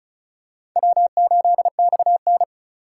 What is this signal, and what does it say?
Morse code, heard as a keyed tone, about 34 wpm, 700 Hz. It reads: W8XN
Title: USA Call Signs - 4 Characters - 4x Character Spacing - 35wpm